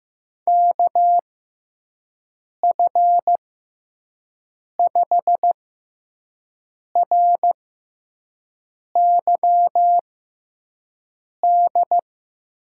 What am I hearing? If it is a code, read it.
Morse code, heard as a keyed tone, about 15 wpm, spaced as for 5 wpm, 700 Hz. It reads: KF5RYD
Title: USA Call Signs - 6x Character Spacing - 15wpm